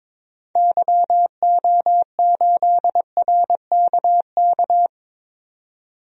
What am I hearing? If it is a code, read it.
Morse code, heard as a keyed tone, about 22 wpm, 700 Hz. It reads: YO8RKK
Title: DXCC-100 Call Signs - 6 Characters - 22wpm